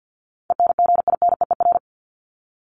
Morse code, heard as a keyed tone, about 50 wpm, 700 Hz. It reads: ENGINEER
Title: ING Suffix - Top 100 Words with Base Words 50wpm